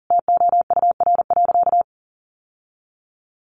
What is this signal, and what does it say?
Morse code, heard as a keyed tone, about 40 wpm, 700 Hz. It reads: TOUR.
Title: Sentences from Top 1400 Words 40wpm